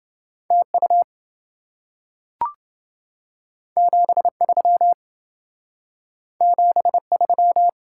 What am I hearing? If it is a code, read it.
Morse code, heard as a keyed tone, about 27 wpm, 700 Hz. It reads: TU 73 73